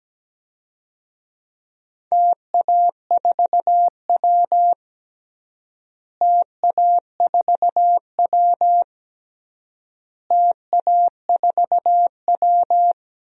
Morse code, heard as a keyed tone, about 17 wpm, 700 Hz. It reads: TA4W TA4W TA4W
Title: DXCC-100 Call Signs - 4 Characters - Repeated 3x - 17wpm